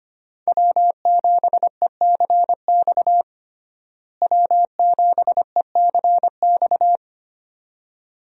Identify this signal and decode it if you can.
Morse code, heard as a keyed tone, about 25 wpm, 700 Hz. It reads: W7ECX W7ECX
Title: USA Call Signs - 5 Characters - Repeated 2x - 25wpm